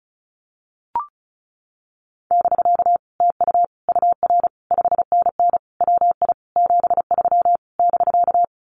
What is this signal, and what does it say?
Morse code, heard as a keyed tone, about 34 wpm, 700 Hz. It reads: <BK> TU UR 5NN WI 73 <BK>
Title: Common QSO Phrases Encore 35wpm